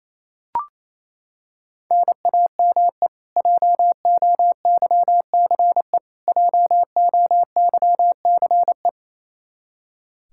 Morse code, heard as a keyed tone, about 28 wpm, 700 Hz. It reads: NAME JOYCE JOYCE
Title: Practice QSO - Exchange 1 - Beginning - 28wpm